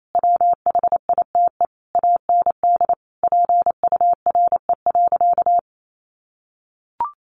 Morse code, heard as a keyed tone, about 28 wpm, 700 Hz. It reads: WHITE AND PURE.
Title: Sentences from Top 1500 Words 28wpm